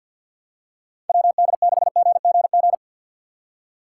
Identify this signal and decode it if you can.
Morse code, heard as a keyed tone, about 50 wpm, 690 Hz. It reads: WD6GGG
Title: USA Call Signs - 4x Character Spacing - 50wpm